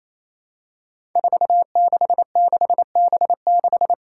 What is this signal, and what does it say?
Morse code, heard as a keyed tone, about 27 wpm, 700 Hz. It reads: 466B6
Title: Five Challenging Letter-Number Combinations 28wpm